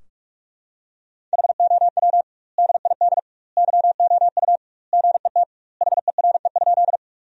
Morse code, heard as a keyed tone, about 46 wpm, 700 Hz. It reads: HOW DID YOU GET HERE?
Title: 2x Word-Spacing - Sentences from Top 200 Words - 45wpm